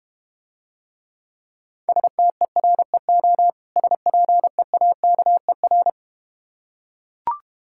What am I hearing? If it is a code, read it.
Morse code, heard as a keyed tone, about 32 wpm, 700 Hz. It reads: STEREO SPEAKER